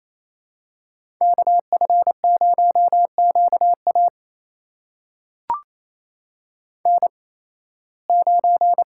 Morse code, heard as a keyed tone, about 28 wpm, spaced as for 18 wpm, 700 Hz. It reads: KF0QA N9
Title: USA Call Signs - 5 Characters - 8x Character Spacing - 28wpm